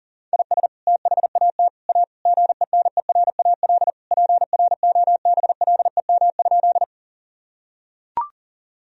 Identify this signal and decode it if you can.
Morse code, heard as a keyed tone, about 40 wpm, 700 Hz. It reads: IS THAT A GENERAL PROBLEM?
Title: Sentences from Top 1000 Words - Review Part 3 - 40wpm